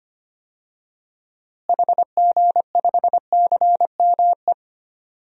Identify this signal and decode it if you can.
Morse code, heard as a keyed tone, about 25 wpm, 700 Hz. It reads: HG5CME
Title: DXCC-100 Call Signs - 6 Characters - Repeated 3x - 25wpm